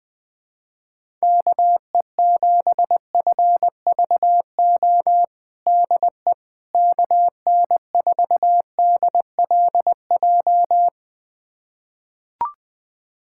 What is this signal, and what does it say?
Morse code, heard as a keyed tone, about 20 wpm, 700 Hz. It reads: KE7FVO DE KN4DLJ
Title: USA Callsign DE Callsign - 6 Characters - 20wpm